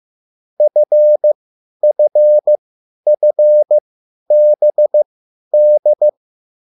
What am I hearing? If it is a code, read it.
Morse code, heard as a keyed tone, about 15 wpm, spaced as for 10 wpm, 600 Hz. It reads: FFFBD